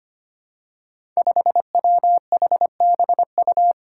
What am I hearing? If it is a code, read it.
Morse code, heard as a keyed tone, about 25 wpm, 700 Hz. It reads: 5WHBU
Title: Five Challenging Letter-Number Combinations 25wpm